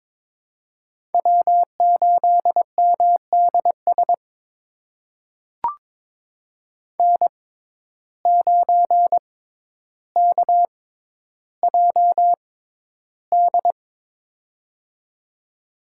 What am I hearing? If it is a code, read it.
Morse code, heard as a keyed tone, about 22 wpm, spaced as for 11 wpm, 700 Hz. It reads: W8MDS N9KJD
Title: USA Call Signs - 5 Characters - 6x Character Spacing - 22wpm